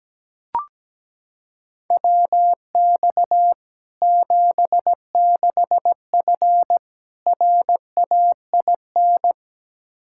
Morse code, heard as a keyed tone, about 17 wpm, 700 Hz. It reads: WX 76F RAIN
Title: Practice QSO - Exchange 2 - Info - 17wpm